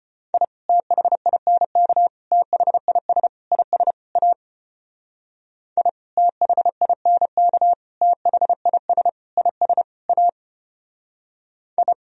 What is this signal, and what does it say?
Morse code, heard as a keyed tone, about 29 wpm, 700 Hz. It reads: I THINK THIS IS A I THINK THIS IS A I
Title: Speed-Racing - Sets of 5 Words 25wpm